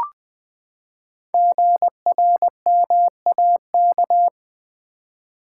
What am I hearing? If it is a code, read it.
Morse code, heard as a keyed tone, about 20 wpm, 700 Hz. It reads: GRMAK
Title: Code Groups - 5 Letters - 20wpm